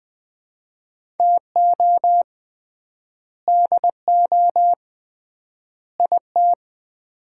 Morse code, heard as a keyed tone, about 20 wpm, 700 Hz. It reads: TO DO IT
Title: 3x Word-Spacing - Sets of 3 Words - 20wpm